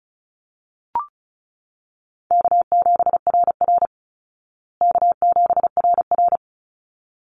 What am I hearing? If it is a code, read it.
Morse code, heard as a keyed tone, about 34 wpm, 700 Hz. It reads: K7RR K7RR
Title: Easy Call Signs Repeated 2x - 35wpm